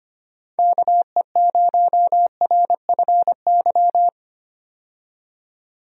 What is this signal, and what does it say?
Morse code, heard as a keyed tone, about 25 wpm, 700 Hz. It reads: KE0RFY
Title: USA Call Signs - 6 Characters - 25wpm